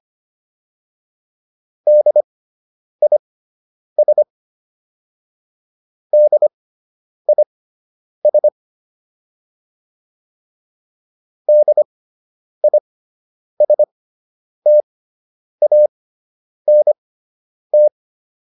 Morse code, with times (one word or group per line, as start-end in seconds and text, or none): 1.9-4.2 DIS
6.1-8.5 DIS
11.5-17.9 DISTANT